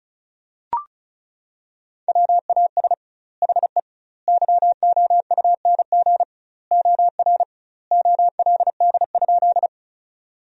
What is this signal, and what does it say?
Morse code, heard as a keyed tone, about 36 wpm, 700 Hz. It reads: WAS HE YOUNG OR OLD?
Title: 2x Word-Spacing - Sentences from Top 400 Words - 35wpm